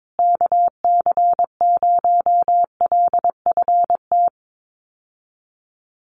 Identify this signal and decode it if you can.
Morse code, heard as a keyed tone, about 22 wpm, 700 Hz. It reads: KC0LFT
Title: USA Call Signs Repeated 2x 22wpm